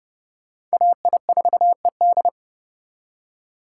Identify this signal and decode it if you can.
Morse code, heard as a keyed tone, about 37 wpm, 700 Hz. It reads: AI4ED
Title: Speed Racing - USA 2x2 Call Signs 30wpm